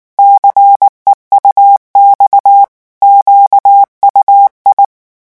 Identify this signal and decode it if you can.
Morse code, heard as a keyed tone, about 19 wpm, 800 Hz. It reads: CEUX QUI
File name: morse mp3 / revision9 citations humoristiques vitesse 18 mots minute